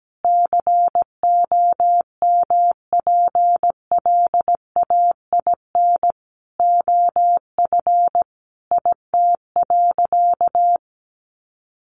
Morse code, heard as a keyed tone, about 17 wpm, 700 Hz. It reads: COMPLAIN OF IT.
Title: Sentences from Top 1300 Words 17wpm